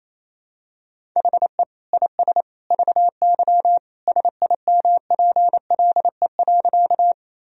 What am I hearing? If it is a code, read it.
Morse code, heard as a keyed tone, about 28 wpm, 700 Hz. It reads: HE IS VY SIMPLE.
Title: Sentences from Top 400 Words and Abbreviations 28wpm